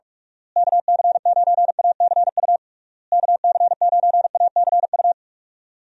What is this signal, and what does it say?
Morse code, heard as a keyed tone, about 46 wpm, 700 Hz. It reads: KC9ACU KC9ACU
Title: USA Call Signs - 6 Characters - Repeated 3x - 45wpm